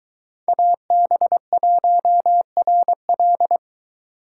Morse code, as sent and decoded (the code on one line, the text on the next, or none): .- -... .---- .-. .-..
AB1RL